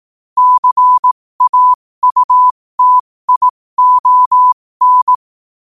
Morse code, heard as a keyed tone, about 18 wpm, spaced as for 15 wpm, 1000 Hz. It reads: CAUTION